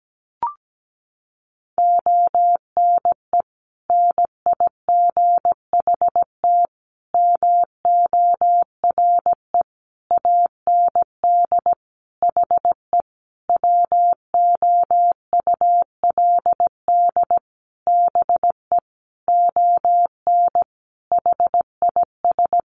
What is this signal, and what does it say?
Morse code, heard as a keyed tone, about 17 wpm, 700 Hz. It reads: ONE NIGHT MORE AND HE WOULD BE ON HIS
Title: Sentences from Top 300 Words 17wpm